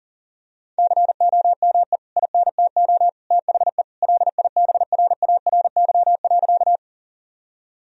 Morse code, heard as a keyed tone, about 40 wpm, 700 Hz. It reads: COME INTO THE LIBRARY.
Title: Sentences from Top 1000 Words - Review Part 4 - 40wpm